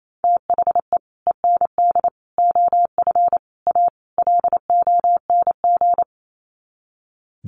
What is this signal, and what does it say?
Morse code, heard as a keyed tone, about 28 wpm, 700 Hz. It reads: THE END OF A LONG
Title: Sets of 5 Words Encore2 28wpm